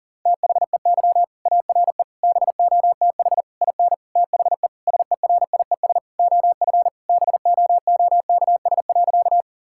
Morse code, heard as a keyed tone, about 37 wpm, 700 Hz. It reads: THEY ARE BOTH IN THE SERIES OF BOOKS.
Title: Sentences from Top 1000 Words - Review Part 5 - 40wpm